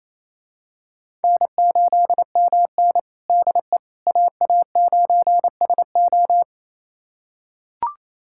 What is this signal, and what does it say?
Morse code, heard as a keyed tone, about 27 wpm, 700 Hz. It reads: N8MN DE AA9SO
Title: USA Callsign DE Callsign 28wpm